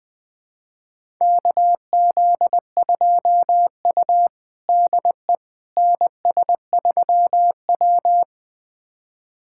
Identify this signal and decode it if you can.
Morse code, heard as a keyed tone, about 20 wpm, 700 Hz. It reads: KZ2U DE NS3W
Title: USA Callsign DE Callsign - 4 Characters - 20wpm